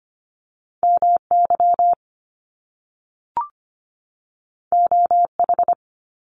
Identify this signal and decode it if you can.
Morse code, heard as a keyed tone, about 25 wpm, 700 Hz. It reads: MY OH
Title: Top Two Letter Combinations in Words 25wpm